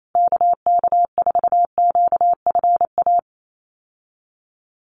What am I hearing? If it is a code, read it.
Morse code, heard as a keyed tone, about 28 wpm, 700 Hz. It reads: KK4QFA
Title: USA Call Signs Repeated 3x 28wpm